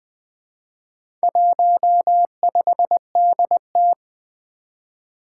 Morse code, heard as a keyed tone, about 23 wpm, 700 Hz. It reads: 15DT
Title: Speed-Racing - Four Letter-Number Combinations - 20wpm